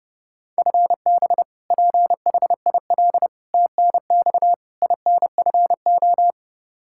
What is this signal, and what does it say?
Morse code, heard as a keyed tone, about 31 wpm, 700 Hz. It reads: FB PHIL TNX INFO
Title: Practice QSO - Exchange 3 - More Info - 30wpm